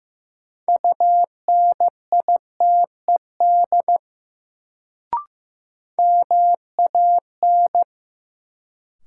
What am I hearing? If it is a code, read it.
Morse code, heard as a keyed tone, about 15 wpm, 700 Hz. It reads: UNITED MAN
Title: Sets of 2 Words - New Word Drill - 15wpm